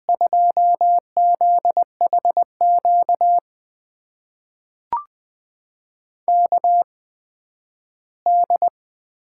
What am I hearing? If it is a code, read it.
Morse code, heard as a keyed tone, about 20 wpm, spaced as for 12 wpm, 700 Hz. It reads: 2ZHQ KD
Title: USA Call Signs - 6 Characters - 8x Character Spacing - 20wpm